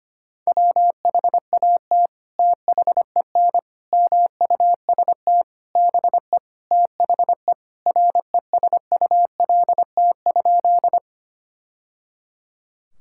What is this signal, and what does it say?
Morse code, heard as a keyed tone, about 25 wpm, 700 Hz. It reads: WHAT THEN MUST BE THE RESULT?
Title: Sentences from Top 1000 Words - Review Part 5 - 25wpm